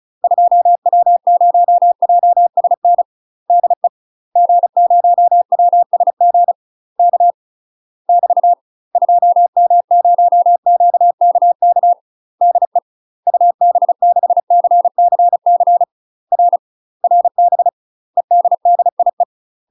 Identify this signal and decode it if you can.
Morse code, heard as a keyed tone, about 36 wpm, 700 Hz. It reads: 2W0JSN DE G0WSG K <BT> 2M0QKK DE UB6CCC R RB EDDIE
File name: UK MASTER QSOs TRAINING / 35WPM / UK MASTER QSOs 0010